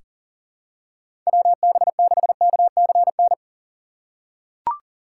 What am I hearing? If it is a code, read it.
Morse code, heard as a keyed tone, about 38 wpm, 700 Hz. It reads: WB6KCN
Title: USA Call Signs - 6x Character Spacing - 40wpm